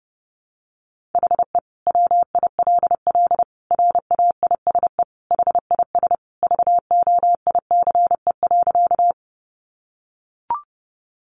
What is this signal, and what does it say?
Morse code, heard as a keyed tone, about 30 wpm, 700 Hz. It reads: HE WILL RAISE HIS VOICE.
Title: Sentences from Top 1400 Words 30wpm